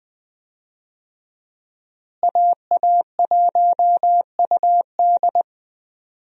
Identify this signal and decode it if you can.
Morse code, heard as a keyed tone, about 21 wpm, 700 Hz. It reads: AA1UD
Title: USA Call Signs - 5 Characters - Repeated 2x - 20wpm